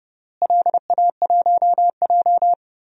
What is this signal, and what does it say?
Morse code, heard as a keyed tone, about 32 wpm, 700 Hz. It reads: LA1J
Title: DXCC-100 Call Signs - 4 Characters - Repeated 3x - 30wpm